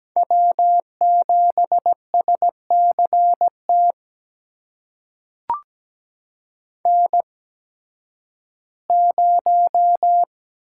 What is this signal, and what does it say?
Morse code, heard as a keyed tone, about 17 wpm, spaced as for 11 wpm, 700 Hz. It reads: W7SCT N0
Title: USA Call Signs - 5 Characters - 8x Character Spacing - 17wpm